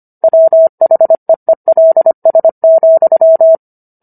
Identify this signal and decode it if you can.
Morse code, heard as a keyed tone, about 25 wpm, 650 Hz. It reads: WHEELS,